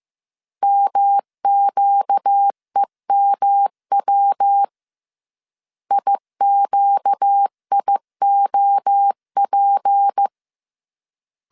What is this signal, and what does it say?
Morse code, heard as a keyed tone, about 15 wpm, 800 Hz. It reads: MQEMW IQIOP
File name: CW / DET / 1 6 1500C S C 14wpm 2951